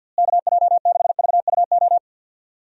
Morse code, heard as a keyed tone, about 45 wpm, 700 Hz. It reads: KJ6VUO
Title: USA Call Signs - 6 Characters - Repeated 3x - 50wpm